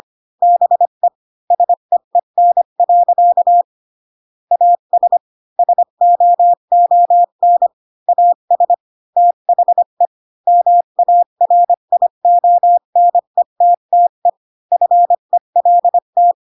Morse code, heard as a keyed tone, about 25 wpm, 700 Hz. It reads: BE SEEN. AS SOON AS THE MARIONETTE FELT
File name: Pinocchio / Pinocchio ch8 20 25 ik1hge